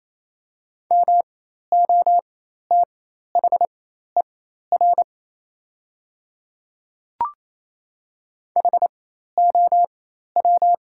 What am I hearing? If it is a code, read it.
Morse code, heard as a keyed tone, about 28 wpm, spaced as for 13 wpm, 700 Hz. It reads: MOTHER HOW
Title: 4x Character Spacing - Top 300 Words - 28wpm